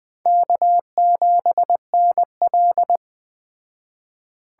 Morse code, heard as a keyed tone, about 18 wpm, 700 Hz. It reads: K7NL
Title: Speed Racing - USA 1x2 Call Signs 15wpm